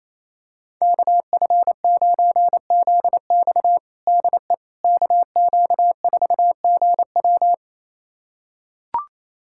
Morse code, heard as a keyed tone, about 28 wpm, 700 Hz. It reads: KF9ZX DE KQ4GW
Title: USA Callsign DE Callsign - 5 Characters - 28wpm